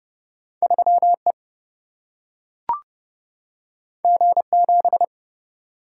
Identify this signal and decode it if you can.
Morse code, heard as a keyed tone, about 30 wpm, 700 Hz. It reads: 3E G7
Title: Single-Letter Single-Number 30wpm